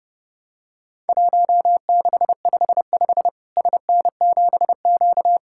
Morse code, heard as a keyed tone, about 32 wpm, 700 Hz. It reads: 1655 SN7Q
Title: Worked All Europe DX Contest - QTC Exchange - 30wpm